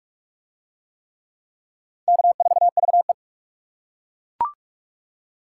Morse code, heard as a keyed tone, about 41 wpm, 700 Hz. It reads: K4VE